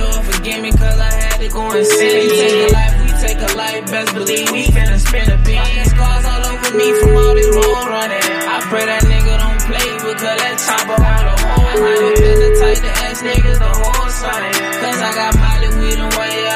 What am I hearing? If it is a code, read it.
Ringback tone (Europe, 425 Hz), with noise.